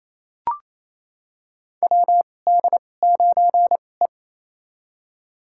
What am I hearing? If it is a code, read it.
Morse code, heard as a keyed tone, about 28 wpm, spaced as for 20 wpm, 700 Hz. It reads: WD9E
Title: USA Call Signs - 2x Character Spacing - 28wpm